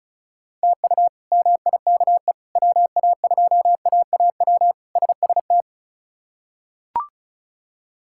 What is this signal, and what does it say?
Morse code, heard as a keyed tone, about 36 wpm, 700 Hz. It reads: TU MIKE WA2AAW SST
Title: K1USN SST QSO Practice - 35wpm